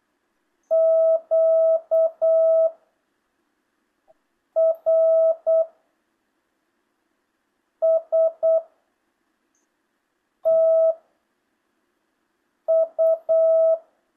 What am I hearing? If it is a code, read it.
Morse code, heard as a keyed tone, about 8 wpm, 650 Hz. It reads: Q R S T U
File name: jh8cbh / 2025cw